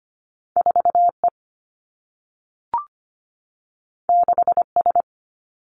Morse code, heard as a keyed tone, about 25 wpm, 700 Hz. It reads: 4E 6S
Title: Two Challenging Letter-Number Combinations 25wpm